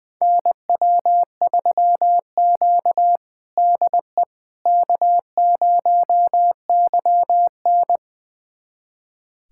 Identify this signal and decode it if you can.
Morse code, heard as a keyed tone, about 20 wpm, 700 Hz. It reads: NW3Q DE K0YN
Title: USA Callsign DE Callsign - 4 Characters - 20wpm